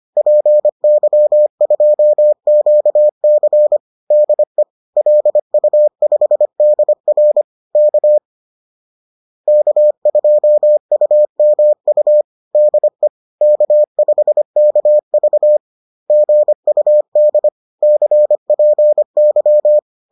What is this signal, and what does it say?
Morse code, heard as a keyed tone, about 25 wpm, 600 Hz. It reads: PY2QC DE LU5DR K K2UMU DE K5KV GUD CPY